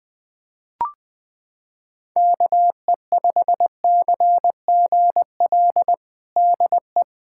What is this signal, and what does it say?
Morse code, heard as a keyed tone, about 21 wpm, 700 Hz. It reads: KE5CGL DE
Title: USA Callsign DE Callsign - 6 Characters - 20wpm